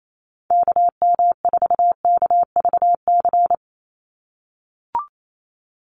Morse code, heard as a keyed tone, about 28 wpm, 700 Hz. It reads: KM4KVC
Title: USA Call Signs 28wpm